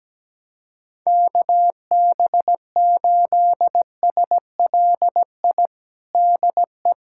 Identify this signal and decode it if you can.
Morse code, heard as a keyed tone, about 17 wpm, 700 Hz. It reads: KB8SLI DE
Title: USA Callsign DE Callsign - 6 Characters - 17wpm